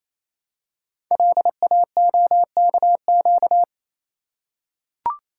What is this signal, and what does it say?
Morse code, heard as a keyed tone, about 28 wpm, 700 Hz. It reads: LAOKQ
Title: Code Groups - 5 Letters - 28wpm